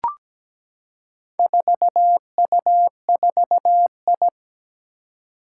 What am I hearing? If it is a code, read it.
Morse code, heard as a keyed tone, about 17 wpm, 700 Hz. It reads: 4U4I
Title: Four Challenging Letter-Number Combinations 17wpm